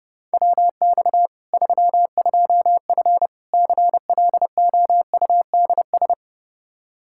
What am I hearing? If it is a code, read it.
Morse code, heard as a keyed tone, about 30 wpm, 700 Hz. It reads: WX 32F CLOUDS
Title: Practice QSO - Exchange 2 - Info - 30wpm